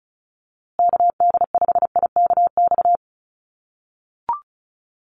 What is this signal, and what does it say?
Morse code, heard as a keyed tone, about 34 wpm, 700 Hz. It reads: KD5IKX
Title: USA Call Signs Repeated 2x 35wpm